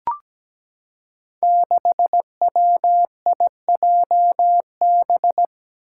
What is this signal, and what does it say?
Morse code, heard as a keyed tone, about 17 wpm, 700 Hz. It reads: 6WIJB
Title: Five Challenging Letter-Number Combinations 17wpm